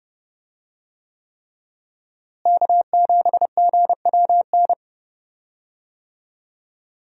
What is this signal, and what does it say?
Morse code, heard as a keyed tone, about 31 wpm, 700 Hz. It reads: K7GWN
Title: USA Call Signs - 5 Characters - Repeated 3x - 30wpm